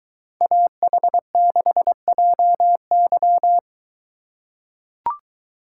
Morse code, heard as a keyed tone, about 23 wpm, 700 Hz. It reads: AH6JY